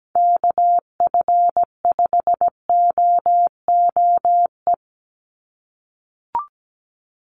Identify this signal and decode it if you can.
Morse code, heard as a keyed tone, about 17 wpm, 700 Hz. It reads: KF5OOE
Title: USA Call Signs 17wpm